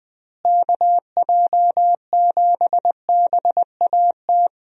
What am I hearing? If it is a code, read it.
Morse code, heard as a keyed tone, about 20 wpm, 700 Hz. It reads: KJ7BAT
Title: USA Call Signs - 6 Characters - Repeated 3x - 20wpm